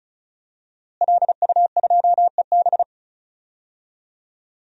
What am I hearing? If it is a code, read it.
Morse code, heard as a keyed tone, about 34 wpm, 700 Hz. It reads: LU2EB
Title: DXCC-100 Call Signs - 5 Characters - Repeated 3x - 35wpm